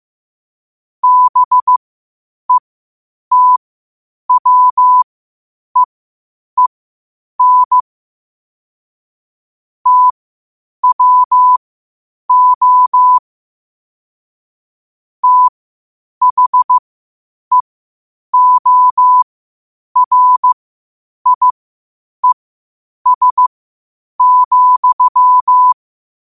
Morse code, heard as a keyed tone, about 15 wpm, spaced as for 8 wpm, 1000 Hz. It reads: BETWEEN TWO THEORIES,
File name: qotdfiles / 8wpm